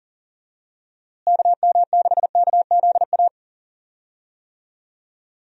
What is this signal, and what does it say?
Morse code, heard as a keyed tone, about 40 wpm, 700 Hz. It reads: KM6KZA
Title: USA Call Signs - 6 Characters - Repeated 2x - 40wpm